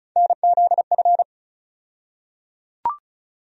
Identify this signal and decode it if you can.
Morse code, heard as a keyed tone, about 34 wpm, 700 Hz. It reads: NZF
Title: USA Call Sign Postfixes - 3 Letters 35wpm